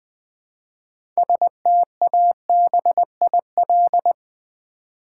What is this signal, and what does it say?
Morse code, heard as a keyed tone, about 20 wpm, 700 Hz. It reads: STABIL
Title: Build A Word - 3 Letters At A Time - 9-Letter Words - 20wpm